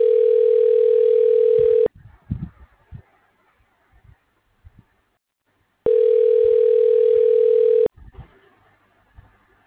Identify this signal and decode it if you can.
Ringback tone (North America), 440 + 480 Hz.
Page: No Answer call. Disconnected automatically